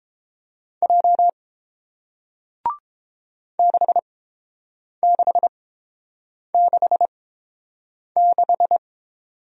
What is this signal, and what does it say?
Morse code, heard as a keyed tone, about 31 wpm, spaced as for 7 wpm, 700 Hz. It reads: J 6666